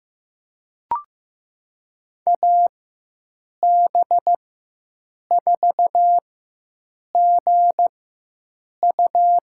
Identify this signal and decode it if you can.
Morse code, heard as a keyed tone, about 15 wpm, spaced as for 7 wpm, 700 Hz. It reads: AB4GU
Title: USA Call Signs - 5 Characters - 4x Character Spacing - 15wpm